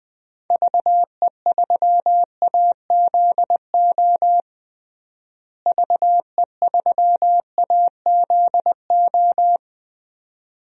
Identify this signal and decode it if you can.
Morse code, heard as a keyed tone, about 20 wpm, 700 Hz. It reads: VE3AZO VE3AZO
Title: DXCC-100 Call Signs - 6 Characters - Repeated 2x - 20wpm